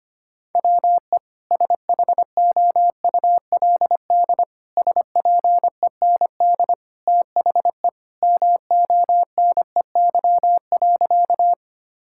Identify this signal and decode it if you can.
Morse code, heard as a keyed tone, about 25 wpm, 700 Hz. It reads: WE SHOULD SPEND THE MONEY.
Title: Sentences from Top 1000 Words - Review Part 5 - 25wpm